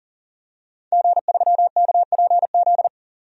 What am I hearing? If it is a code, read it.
Morse code, heard as a keyed tone, about 38 wpm, 700 Hz. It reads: G3KPZ
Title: DXCC-100 Call Signs - 5 Characters - 40wpm